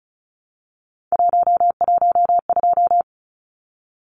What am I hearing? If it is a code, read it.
Morse code, heard as a keyed tone, about 34 wpm, 700 Hz. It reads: 112
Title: Numbers - 3-Digit 35wpm